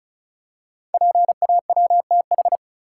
Morse code, heard as a keyed tone, about 36 wpm, 700 Hz. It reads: PAWTH
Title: Code Groups - 5 Letters - 35wpm